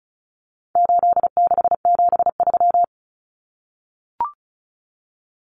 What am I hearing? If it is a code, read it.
Morse code, heard as a keyed tone, about 34 wpm, 700 Hz. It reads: 8673
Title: Common Cut Numbers - 4-Digit 35wpm